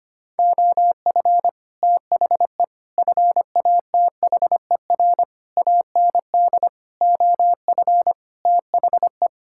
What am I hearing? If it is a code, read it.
Morse code, heard as a keyed tone, about 25 wpm, 700 Hz. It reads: OF THE FATHER AND OF THE
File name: Sets of 9 Words - 25wpm